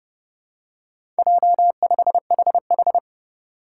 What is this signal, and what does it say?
Morse code, heard as a keyed tone, about 30 wpm, 700 Hz. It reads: J5HH
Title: Four Challenging Letter-Number Combinations 30wpm